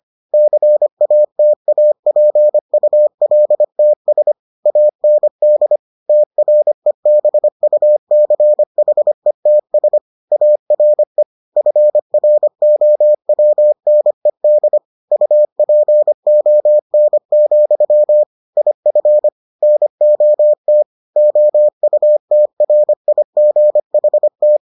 Morse code, heard as a keyed tone, about 25 wpm, 600 Hz. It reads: CATAPULTS AND TREBUCHETS ARE FROWNED UPON, IF NOT OUTRIGHT